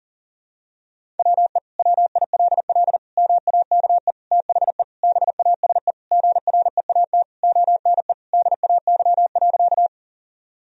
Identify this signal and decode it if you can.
Morse code, heard as a keyed tone, about 40 wpm, 700 Hz. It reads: WE WILL MAKE THE BASE GREAT ONE DAY.
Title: Sentences from Top 1000 Words - Review Part 1 - 40wpm